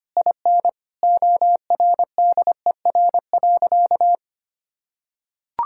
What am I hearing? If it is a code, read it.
Morse code, heard as a keyed tone, about 25 wpm, 700 Hz. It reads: IN ORDER.
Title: Sentences from Top 1000 Words - Review Part 6 - 25wpm